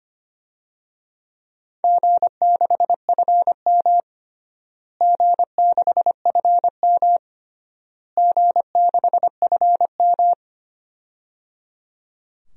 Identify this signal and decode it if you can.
Morse code, heard as a keyed tone, about 25 wpm, 700 Hz. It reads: G6FM G6FM G6FM
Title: DXCC-100 Call Signs - 4 Characters - Repeated 3x - 25wpm